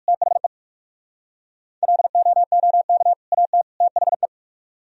Morse code, heard as a keyed tone, about 58 wpm, 700 Hz. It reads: THE LOOK AT THE